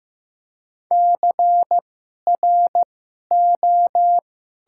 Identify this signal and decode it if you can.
Morse code, heard as a keyed tone, about 15 wpm, spaced as for 11 wpm, 700 Hz. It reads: CRO
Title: 2x Character Spacing - Top 500 Words - 15wpm